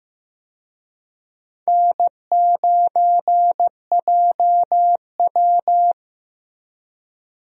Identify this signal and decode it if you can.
Morse code, heard as a keyed tone, about 15 wpm, 700 Hz. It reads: N9JW
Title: USA Call Signs - 4 Characters - Repeated 3x - 15wpm